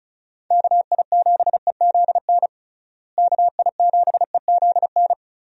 Morse code, heard as a keyed tone, about 34 wpm, 700 Hz. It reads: KI7EZN KI7EZN
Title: USA Call Signs - 6 Characters - Repeated 2x - 35wpm